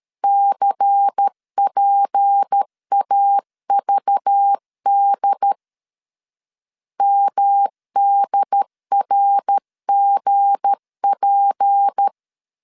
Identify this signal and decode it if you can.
Morse code, heard as a keyed tone, about 13 wpm, 800 Hz. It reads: CPAVD MDRGP
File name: LEZIONI / 7 GF9Z / 7 G 12wpm 2055